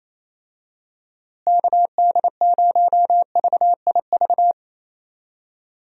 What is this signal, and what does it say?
Morse code, heard as a keyed tone, about 27 wpm, 700 Hz. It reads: KD0VIV
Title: USA Call Signs - 6 Characters - Repeated 3x - 28wpm